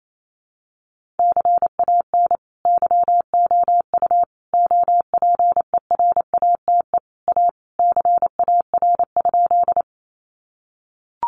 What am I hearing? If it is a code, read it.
Morse code, heard as a keyed tone, about 28 wpm, 700 Hz. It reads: CAN YOU OPERATE A CAR?
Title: Sentences from Top 1500 Words 28wpm